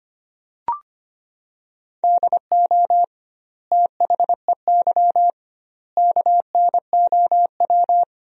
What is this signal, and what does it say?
Morse code, heard as a keyed tone, about 25 wpm, 700 Hz. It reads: DO THEY KNOW
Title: 2x Word-Spacing - Sentences from Top 100 Words - 25wpm